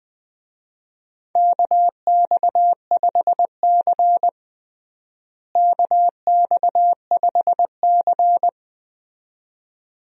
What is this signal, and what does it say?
Morse code, heard as a keyed tone, about 20 wpm, 700 Hz. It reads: KX5C KX5C
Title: USA Call Signs - 4 Characters - Repeated 2x - 20wpm